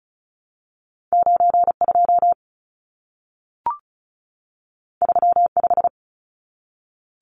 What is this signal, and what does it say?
Morse code, heard as a keyed tone, about 36 wpm, 700 Hz. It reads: 92 35
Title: Numbers - 2-Digit 35wpm